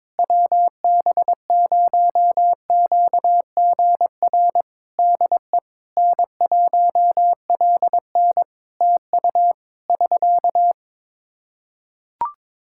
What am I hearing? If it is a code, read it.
Morse code, heard as a keyed tone, about 22 wpm, 700 Hz. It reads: WB0QGR DE N1LN TU <SK>
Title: Practice QSO - Exchange 4 - Ending - 22wpm